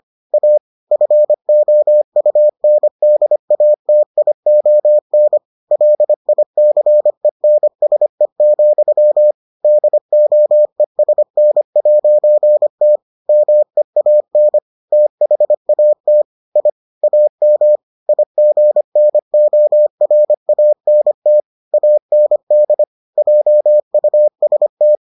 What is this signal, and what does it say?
Morse code, heard as a keyed tone, about 25 wpm, 600 Hz. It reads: A FOUNDATION LICENSE, DOESN'T MEAN THAT I AM IGNORANT AND JUST